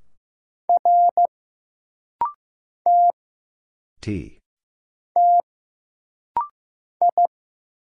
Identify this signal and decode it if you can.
Morse code, heard as a keyed tone, about 15 wpm, 700 Hz. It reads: R T T I
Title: Single Letter-Number Alt - Rapid-Fire 15wpm